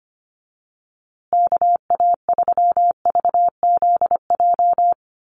Morse code, heard as a keyed tone, about 25 wpm, 700 Hz. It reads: KA3VZJ
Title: USA Call Signs Repeated 3x 25wpm